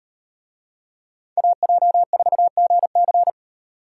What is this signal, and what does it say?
Morse code, heard as a keyed tone, about 37 wpm, 700 Hz. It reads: AJ4GC